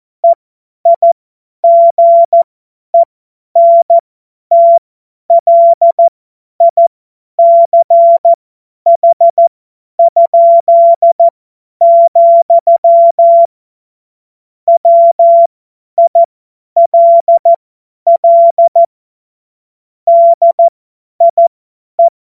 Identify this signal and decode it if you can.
Morse code, heard as a keyed tone, about 14 wpm, spaced as for 10 wpm, 680 Hz. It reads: EIGENTLICH?, WILL DIE